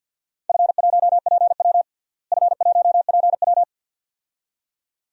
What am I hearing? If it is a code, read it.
Morse code, heard as a keyed tone, about 52 wpm, 700 Hz. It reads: F1PW F1PW
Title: DXCC-100 Call Signs - 4 Characters - Repeated 2x - 50wpm